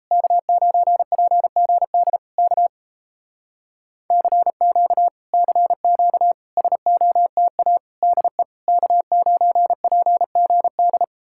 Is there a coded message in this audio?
Yes: Morse code from 0.1 to 11.0 s, heard as a keyed tone, about 38 wpm, 700 Hz. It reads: K9PGD K CQ CQ SOTA DE K9PGD